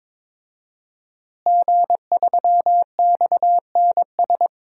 Morse code, heard as a keyed tone, about 22 wpm, 700 Hz. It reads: G3XNS
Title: DXCC-100 Call Signs - 5 Characters - Repeated 3x - 22wpm